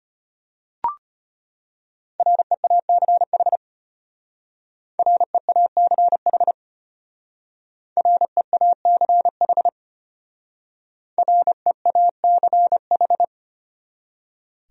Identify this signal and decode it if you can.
Morse code, heard as a keyed tone, about 33 wpm, 700 Hz. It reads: REACH REACH REACH REACH